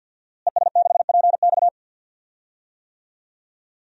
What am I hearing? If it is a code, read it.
Morse code, heard as a keyed tone, about 53 wpm, 700 Hz. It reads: ES6PX
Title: DXCC-100 Call Signs - 5 Characters - Repeated 3x - 50wpm